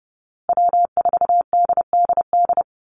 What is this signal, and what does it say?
Morse code, heard as a keyed tone, about 30 wpm, 700 Hz. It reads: W4DDD
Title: Call Signs Repeated 2x - 30wpm